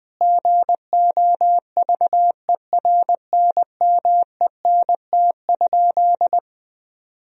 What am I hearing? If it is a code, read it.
Morse code, heard as a keyed tone, about 20 wpm, 700 Hz. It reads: GOVERNMENT?
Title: Sentences from Top 1000 Words - Review Part 4 - 20wpm